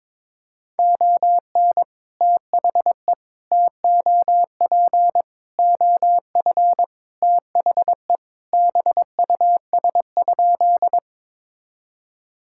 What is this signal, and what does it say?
Morse code, heard as a keyed tone, about 22 wpm, 700 Hz. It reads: ON THE TOP OF THE BUS?
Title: Sentences from Top 1000 Words - Review Part 6 - 22wpm